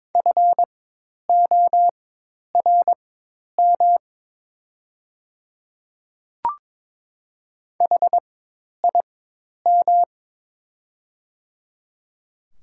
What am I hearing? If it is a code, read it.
Morse code, heard as a keyed tone, about 22 wpm, spaced as for 10 wpm, 700 Hz. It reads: FORM HIM
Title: 4x Character Spacing - Top 200 Words - 22wpm